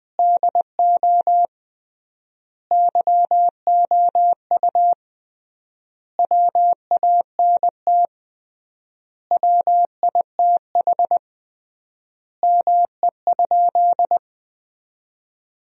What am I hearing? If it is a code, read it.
Morse code, heard as a keyed tone, about 20 wpm, 700 Hz. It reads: DO YOU WANT WITH ME?
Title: 3x Word-Spacing - Sentences from Top 200 Words - 20wpm